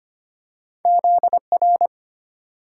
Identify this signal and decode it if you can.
Morse code, heard as a keyed tone, about 25 wpm, 700 Hz. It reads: Z R
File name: Two Letter-Number Combinations - 25wpm